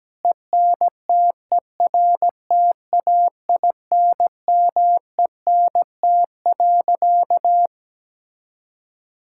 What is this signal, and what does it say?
Morse code, heard as a keyed tone, about 17 wpm, 700 Hz. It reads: ENTERTAINMENT.
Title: Sentences from Top 1000 Words - Review Part 6 - 17wpm